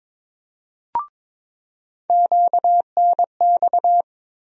Morse code, heard as a keyed tone, about 22 wpm, 700 Hz. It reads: QNX
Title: USA Call Sign Postfixes - 3 Letters 22wpm